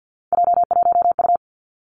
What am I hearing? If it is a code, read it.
Morse code, heard as a keyed tone, about 50 wpm, 700 Hz. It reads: J1U
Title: Three Challenging Letter-Number Combinations 50wpm